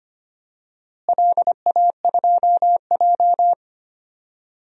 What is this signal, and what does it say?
Morse code, heard as a keyed tone, about 25 wpm, 700 Hz. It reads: LA2J
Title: DXCC-100 Call Signs - 4 Characters - Repeated 3x - 25wpm